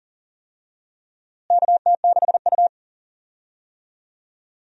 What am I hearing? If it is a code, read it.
Morse code, heard as a keyed tone, about 40 wpm, 700 Hz. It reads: KT6U
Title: USA Call Signs - 4 Characters - Repeated 2x - 40wpm